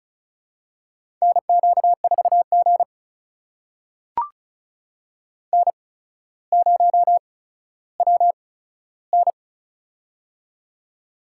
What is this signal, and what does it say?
Morse code, heard as a keyed tone, about 36 wpm, spaced as for 15 wpm, 700 Hz. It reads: NQ4G N0WN
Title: USA Call Signs - 4 Characters - 8x Character Spacing - 35wpm